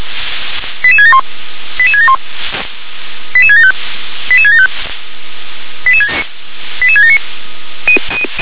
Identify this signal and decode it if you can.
Five-tone selective call (ZVEI), heard as quick runs of five tones, tones of about 70 ms. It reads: 88671 88671 88675 88675 88679
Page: SDR/ZVEI